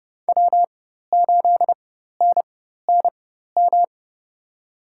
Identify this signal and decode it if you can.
Morse code, heard as a keyed tone, about 30 wpm, spaced as for 14 wpm, 700 Hz. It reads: W8NNM
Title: USA Call Signs - 5 Characters - 4x Character Spacing - 30wpm